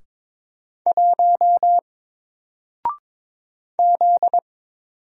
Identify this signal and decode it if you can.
Morse code, heard as a keyed tone, about 22 wpm, 700 Hz. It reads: 1 Z
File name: Single-Letter-Number-22wpm